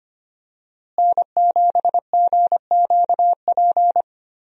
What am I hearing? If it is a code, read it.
Morse code, heard as a keyed tone, about 25 wpm, 700 Hz. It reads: N7GQP
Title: USA Call Signs - 5 Characters - Repeated 3x - 25wpm